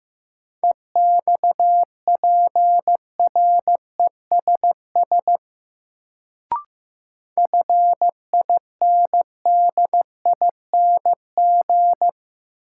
Morse code, heard as a keyed tone, about 15 wpm, 700 Hz. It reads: EXPRESS FINDING
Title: Sentences from Top 1100 Words - New Word Drill Alt - 15wpm